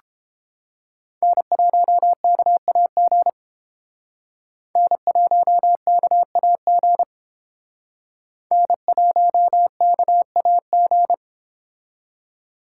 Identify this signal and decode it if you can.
Morse code, heard as a keyed tone, about 29 wpm, 700 Hz. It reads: N1KAG N1KAG N1KAG